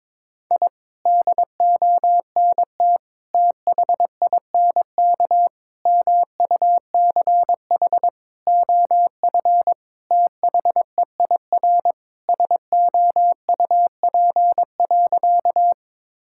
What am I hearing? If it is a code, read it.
Morse code, heard as a keyed tone, about 22 wpm, 700 Hz. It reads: I DONT THINK MUCH OF THEIR SOUP.
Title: Sentences from Top 1000 Words - Review Part 6 - 22wpm